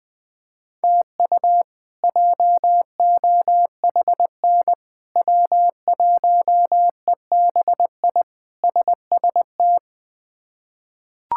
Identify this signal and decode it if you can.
Morse code, heard as a keyed tone, about 20 wpm, 700 Hz. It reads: TU JOHN W1EBI SST
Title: K1USN SST QSO Practice - 20wpm